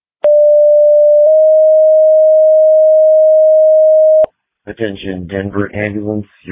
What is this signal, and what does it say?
Two-tone page, heard as a one-second tone, then a three-second tone, about 590 then 620 Hz, with speech. It is heard from 0.2 to 4.3 s.